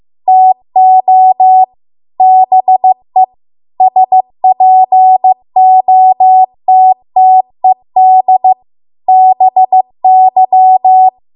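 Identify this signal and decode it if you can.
Morse code, heard as a keyed tone, about 15 wpm, 750 Hz. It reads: TO BE SPOTTED BY
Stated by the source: Morse code practice at 15 wpm from the national amateur radio association's code archive.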